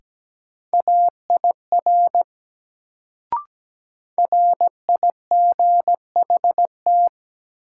Morse code, heard as a keyed tone, about 17 wpm, 700 Hz. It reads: AIR RIGHT
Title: Sentences from Top 200 Words - New Word Drill - 17wpm